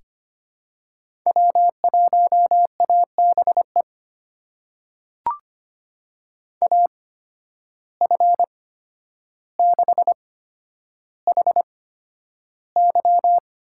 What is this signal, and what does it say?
Morse code, heard as a keyed tone, about 25 wpm, spaced as for 11 wpm, 700 Hz. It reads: W1ABE AF6HY
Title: USA Call Signs - 5 Characters - 8x Character Spacing - 25wpm